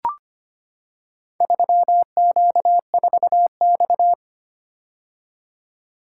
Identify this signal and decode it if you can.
Morse code, heard as a keyed tone, about 25 wpm, 700 Hz. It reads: 3Q4X
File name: practice / 25wpm / Four Letter-Number Combinations - 25wpm